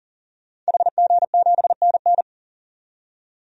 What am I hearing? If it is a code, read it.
Morse code, heard as a keyed tone, about 37 wpm, 700 Hz. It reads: HG7NN
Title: DXCC-100 Call Signs - 5 Characters - 40wpm